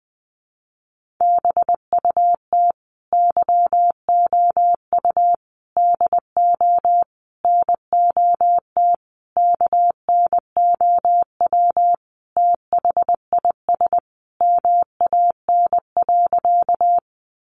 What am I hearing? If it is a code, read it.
Morse code, heard as a keyed tone, about 20 wpm, 700 Hz. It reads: BUT YOU DO NOT KNOW THIS MAN.
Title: Sentences from Top 600 Words 20wpm